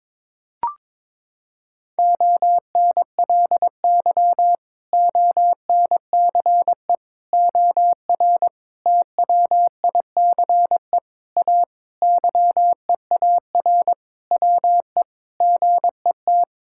Morse code, heard as a keyed tone, about 22 wpm, 700 Hz. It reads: ONLY ONCE OR TWICE A YEAR WE GET
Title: Sentences from Top 1600 Words 22wpm